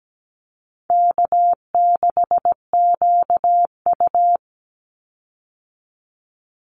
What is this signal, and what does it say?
Morse code, heard as a keyed tone, about 17 wpm, 700 Hz. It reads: K6QU
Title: Easy Call Signs Repeated 3x - 17wpm